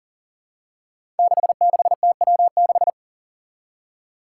Morse code, heard as a keyed tone, about 40 wpm, 700 Hz. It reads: 66TW6
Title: Five Challenging Letter-Number Combinations 40wpm